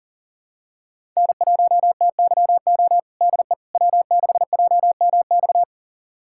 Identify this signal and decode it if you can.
Morse code, heard as a keyed tone, about 40 wpm, 700 Hz. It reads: N1TYO DE W6JMX
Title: USA Callsign DE Callsign - 5 Characters - 40wpm